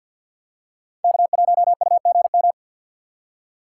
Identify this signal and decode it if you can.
Morse code, heard as a keyed tone, about 52 wpm, 690 Hz. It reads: K1UGM
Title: USA Call Signs - 5 Characters - 6x Character Spacing - 50wpm